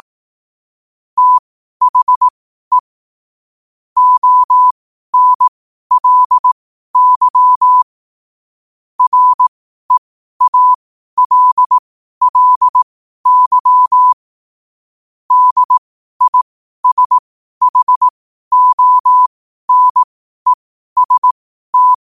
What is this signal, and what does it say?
Morse code, heard as a keyed tone, about 18 wpm, spaced as for 12 wpm, 1000 Hz. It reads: THE ONLY REALLY DISHONEST
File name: qotdfiles / 12wpm